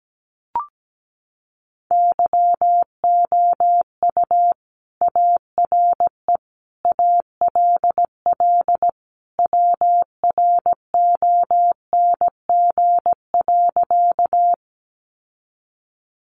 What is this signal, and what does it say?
Morse code, heard as a keyed tone, about 17 wpm, 700 Hz. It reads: YOU ARE ALL WRONG.
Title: Sentences from Top 900 Words 17wpm